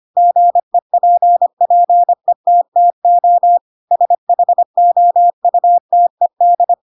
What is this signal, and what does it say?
Morse code, heard as a keyed tone, about 25 wpm, 700 Hz. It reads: GEPPETTO SHOUTED
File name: Pinocchio / Pinocchio ch7 25 25 ik1hge